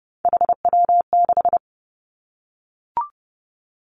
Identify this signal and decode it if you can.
Morse code, heard as a keyed tone, about 32 wpm, 700 Hz. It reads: HW6
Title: Three Challenging Letter-Number Combinations 30wpm